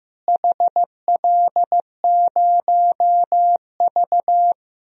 Morse code, heard as a keyed tone, about 15 wpm, 700 Hz. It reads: HL0V
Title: DXCC-100 Call Signs - 4 Characters - Repeated 3x - 15wpm